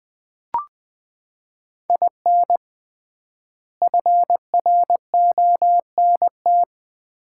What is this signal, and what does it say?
Morse code, heard as a keyed tone, about 20 wpm, 700 Hz. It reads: IN FRONT
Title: 3x Word-Spacing - Sets of 3 Words - 20wpm